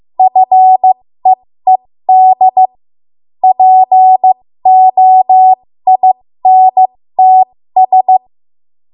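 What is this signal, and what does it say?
Morse code, heard as a keyed tone, about 15 wpm, spaced as for 13 wpm, 750 Hz. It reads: FEED POINTS
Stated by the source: 13 wpm Morse code practice archive transmission.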